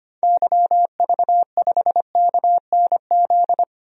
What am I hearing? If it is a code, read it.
Morse code, heard as a keyed tone, about 25 wpm, 700 Hz. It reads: YV5KNZ
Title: DXCC-100 Call Signs - 6 Characters - Repeated 3x - 25wpm